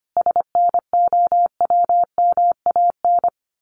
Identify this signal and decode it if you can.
Morse code, heard as a keyed tone, about 25 wpm, 700 Hz. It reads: SNOWMAN
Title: Easy Phrases 25wpm